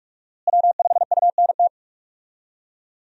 Morse code, heard as a keyed tone, about 46 wpm, 700 Hz. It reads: W5UNT
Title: USA Call Signs - 5 Characters - Repeated 3x - 45wpm